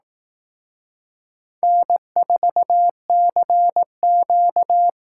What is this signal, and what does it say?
Morse code, heard as a keyed tone, about 18 wpm, 700 Hz. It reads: N4CQ
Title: Speed Racing - USA Call Signs 15wpm